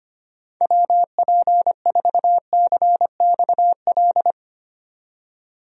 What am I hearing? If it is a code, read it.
Morse code, heard as a keyed tone, about 25 wpm, 700 Hz. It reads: WP4CXL
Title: USA Call Signs - 6 Characters - 25wpm